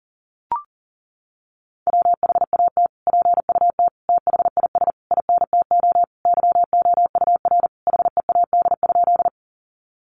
Morse code, heard as a keyed tone, about 37 wpm, 700 Hz. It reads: WHAT PUT THIS INTO YOUR HEAD?
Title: Sentences from Top 600 Words 40wpm